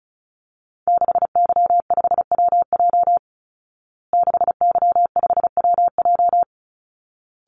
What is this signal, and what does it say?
Morse code, heard as a keyed tone, about 34 wpm, 700 Hz. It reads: 6Y5WJ 6Y5WJ
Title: Call Signs Repeated 3x - 35wpm